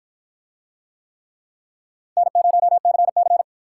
Morse code, heard as a keyed tone, about 44 wpm, 700 Hz. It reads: N0CC